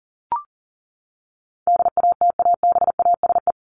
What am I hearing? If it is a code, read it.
Morse code, heard as a keyed tone, about 40 wpm, 700 Hz. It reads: DATABASE
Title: Top 1751-2000 Words 40wpm